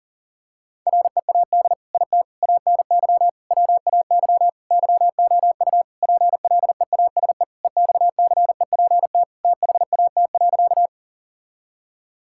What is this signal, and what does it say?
Morse code, heard as a keyed tone, about 37 wpm, 700 Hz. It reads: READ IT ANY WAY YOU PLEASE EXCEPT THAT.
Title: Sentences from Top 1600 Words 40wpm